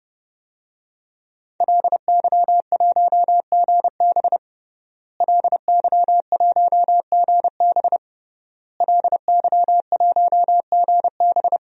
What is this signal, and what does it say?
Morse code, heard as a keyed tone, about 30 wpm, 700 Hz. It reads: LY1GB LY1GB LY1GB
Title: DXCC-100 Call Signs - 5 Characters - Repeated 3x - 30wpm